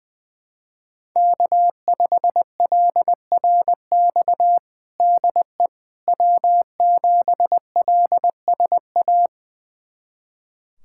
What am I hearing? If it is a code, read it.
Morse code, heard as a keyed tone, about 20 wpm, 700 Hz. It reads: K5LRX DE W7LSA
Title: USA Callsign DE Callsign 20wpm